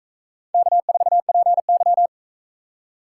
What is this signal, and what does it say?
Morse code, heard as a keyed tone, about 44 wpm, 700 Hz. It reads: K4PY